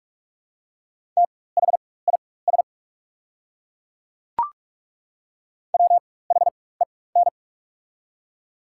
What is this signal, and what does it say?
Morse code, heard as a keyed tone, about 43 wpm, spaced as for 21 wpm, 690 Hz. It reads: THIS WHEN
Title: 4x Character Spacing - Top 200 Words - 45wpm